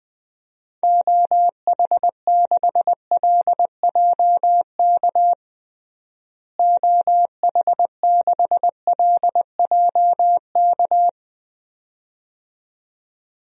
Morse code, heard as a keyed tone, about 20 wpm, 700 Hz. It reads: OH6LJK OH6LJK
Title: DXCC-100 Call Signs - 6 Characters - Repeated 2x - 20wpm